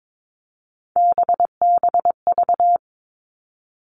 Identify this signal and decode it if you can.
Morse code, heard as a keyed tone, about 22 wpm, 700 Hz. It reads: BBV
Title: All Cut Numbers - 3-Digit 22wpm